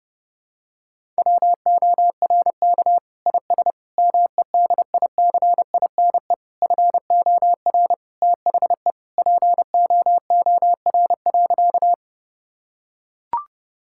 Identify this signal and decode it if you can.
Morse code, heard as a keyed tone, about 30 wpm, 700 Hz. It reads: WORK IS MEDICINE FOR THE POOR.
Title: Sentences from Top 1000 Words - Review Part 6 - 30wpm